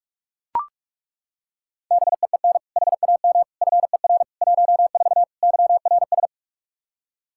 Morse code, heard as a keyed tone, about 41 wpm, 700 Hz. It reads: BEEN HAM FER 14 YRS
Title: Practice QSO - Exchange 3 - More Info - 45wpm